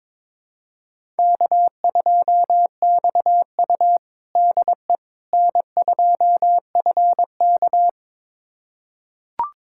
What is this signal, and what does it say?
Morse code, heard as a keyed tone, about 22 wpm, 700 Hz. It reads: K2XU DE N2FK
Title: USA Callsign DE Callsign - 4 Characters - 22wpm